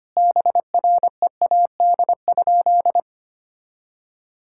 Morse code, heard as a keyed tone, about 25 wpm, 700 Hz. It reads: BREAD?
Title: Sentences from Top 1600 Words 25wpm